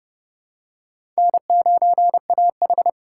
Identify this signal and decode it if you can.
Morse code, heard as a keyed tone, about 30 wpm, 700 Hz. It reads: N9AH
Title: USA Call Signs - 4 Characters - 4x Character Spacing - 30wpm